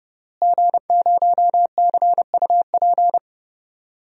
Morse code, heard as a keyed tone, about 30 wpm, 700 Hz. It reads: G0CUP
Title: DXCC-100 Call Signs - 5 Characters - Repeated 3x - 30wpm